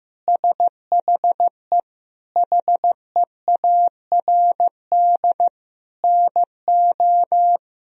Morse code, heard as a keyed tone, about 15 wpm, 700 Hz. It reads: SHE HEARD NO
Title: Sentences from Top 1000 Words - Review Part 1 - 15wpm